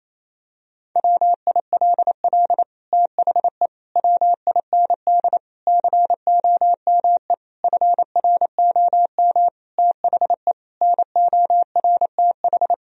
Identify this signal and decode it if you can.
Morse code, heard as a keyed tone, about 28 wpm, 700 Hz. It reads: WILL THE WIND COME FROM THE NORTH
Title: Sentences from Top 1000 Words - Review Part 2 - 28wpm